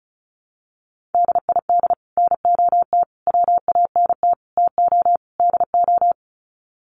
Morse code, heard as a keyed tone, about 34 wpm, 700 Hz. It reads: DID NOT WANT TO DO
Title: Sets of 5 Words Encore 35wpm